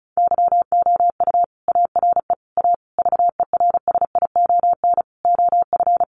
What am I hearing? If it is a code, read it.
Morse code, heard as a keyed tone, about 34 wpm, 700 Hz. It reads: YOU ARE A VERSION OF
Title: Sentences from Top 1400 Words 35wpm